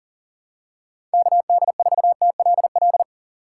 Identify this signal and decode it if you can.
Morse code, heard as a keyed tone, about 40 wpm, 700 Hz. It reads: KD4TLL
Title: USA Call Signs - 6 Characters - 4x Character Spacing - 40wpm